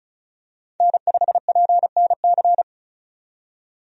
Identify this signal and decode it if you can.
Morse code, heard as a keyed tone, about 36 wpm, 700 Hz. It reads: N5PNC
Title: USA Call Signs - 5 Characters - 35wpm